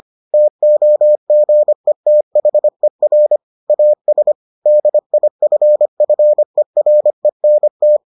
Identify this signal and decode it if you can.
Morse code, heard as a keyed tone, about 25 wpm, 600 Hz. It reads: TOGETHER AS DIFFERENT